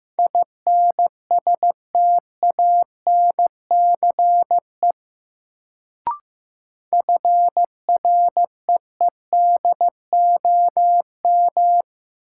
Morse code, heard as a keyed tone, about 15 wpm, 700 Hz. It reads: INSTANCE FREEDOM
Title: Sentences from Top 800 Words - New Word Drill - 15wpm